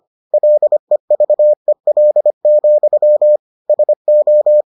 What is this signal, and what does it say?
Morse code, heard as a keyed tone, about 25 wpm, 600 Hz. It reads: LEVEL, SO